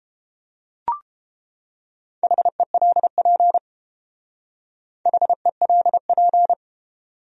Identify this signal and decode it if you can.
Morse code, heard as a keyed tone, about 29 wpm, 700 Hz. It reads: HELP HELP